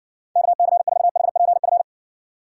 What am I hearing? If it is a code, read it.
Morse code, heard as a keyed tone, about 58 wpm, 700 Hz. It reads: KC4HPF